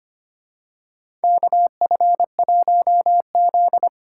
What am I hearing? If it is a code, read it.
Morse code, heard as a keyed tone, about 25 wpm, 700 Hz. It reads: KF1Z
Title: USA Call Signs - 4 Characters - 3x Character Spacing - 25wpm